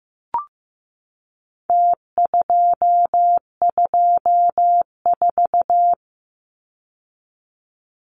Morse code, heard as a keyed tone, about 15 wpm, 700 Hz. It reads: T224
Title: Numbers - Contest Style - 4-Digit 15wpm